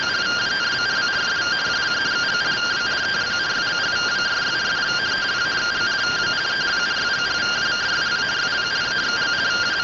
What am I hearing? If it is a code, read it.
Two-tone teleprinter signal, about 170 Hz shift.